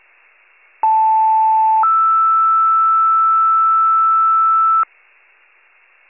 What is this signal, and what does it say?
Two-tone page, heard as a one-second tone, then a three-second tone, about 880 then 1320 Hz. It is heard from 0.8 to 4.8 s.